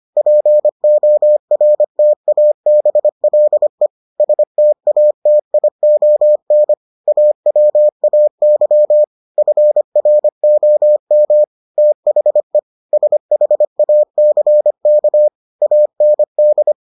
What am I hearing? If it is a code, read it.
Morse code, heard as a keyed tone, about 25 wpm, 600 Hz. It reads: PORTABLE STATION AWAY FROM THE SHACK AND